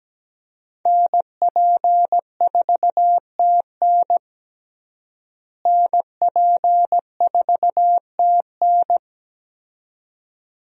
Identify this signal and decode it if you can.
Morse code, heard as a keyed tone, about 17 wpm, 700 Hz. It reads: NP4TN NP4TN
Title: USA Call Signs - 5 Characters - Repeated 2x - 17wpm